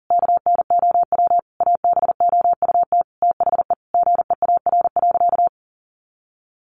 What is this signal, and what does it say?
Morse code, heard as a keyed tone, about 40 wpm, 700 Hz. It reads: KNOW ABOUT THE GEAR.
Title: Sentences from Top 1400 Words 40wpm